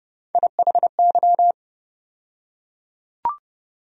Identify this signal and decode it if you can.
Morse code, heard as a keyed tone, about 30 wpm, 700 Hz. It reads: IHY
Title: USA Call Sign Postfixes - 3 Letters 30wpm